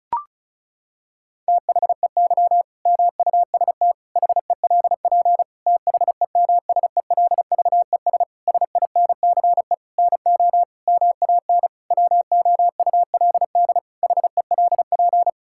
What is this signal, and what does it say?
Morse code, heard as a keyed tone, about 36 wpm, 700 Hz. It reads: THEY MUST HELP THEMSELVES SINCE NO MAN WOULD HELP
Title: Sentences from Top 1000 Words - Review Part 5 - 35wpm